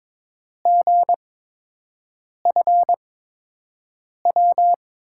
Morse code, heard as a keyed tone, about 21 wpm, spaced as for 6 wpm, 700 Hz. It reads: GFW
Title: USA Call Signs - 5 Characters - 8x Character Spacing - 22wpm